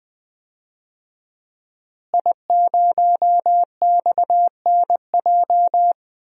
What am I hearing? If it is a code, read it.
Morse code, heard as a keyed tone, about 21 wpm, 700 Hz. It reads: I0XNJ
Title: DXCC-100 Call Signs - 5 Characters - Repeated 2x - 20wpm